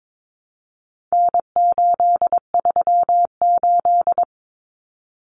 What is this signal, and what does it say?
Morse code, heard as a keyed tone, about 22 wpm, 700 Hz. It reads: N838
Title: Common Cut Numbers - 4-Digit 22wpm